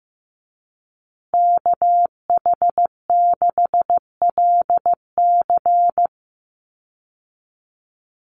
Morse code, heard as a keyed tone, about 15 wpm, 700 Hz. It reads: KH6LC
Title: Call Signs Repeated 3x - 15wpm